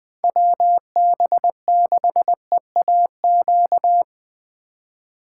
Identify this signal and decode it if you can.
Morse code, heard as a keyed tone, about 20 wpm, 700 Hz. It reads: WB6EAQ